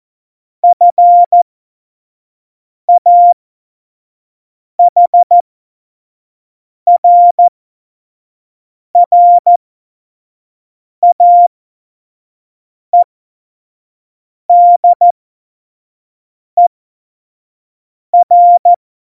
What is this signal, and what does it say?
Morse code, heard as a keyed tone, about 14 wpm, spaced as for 5 wpm, 700 Hz. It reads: FAHRRAEDER
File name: Morsekurs / Witze / Witz 30. Oktober 2018, 25 F70